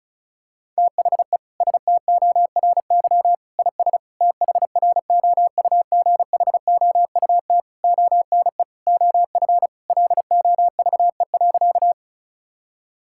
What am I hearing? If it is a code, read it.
Morse code, heard as a keyed tone, about 34 wpm, 690 Hz. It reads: THE STORY IS THROUGHOUT ONE OF LOVE.
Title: Sentences from Top 1000 Words - Review Part 5 - 35wpm